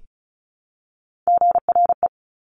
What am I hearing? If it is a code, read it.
Morse code, heard as a keyed tone, about 36 wpm, 700 Hz. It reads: GRE
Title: Top Three Letter Combinations in Words 35wpm